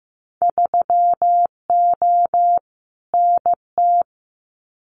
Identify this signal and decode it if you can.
Morse code, heard as a keyed tone, about 15 wpm, 700 Hz. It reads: 3O NT
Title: Winter Field Day Exchanges Encore 15wpm